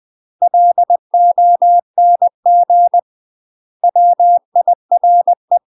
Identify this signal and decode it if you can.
Morse code, heard as a keyed tone, about 20 wpm, 700 Hz. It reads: LONG WIRE
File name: UK MASTER QSOs TRAINING / 20WPM / UK MASTER QSOs 0040